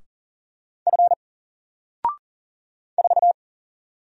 Morse code, heard as a keyed tone, about 40 wpm, 700 Hz. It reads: F 4
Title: Single Letter-Number Alt 40wpm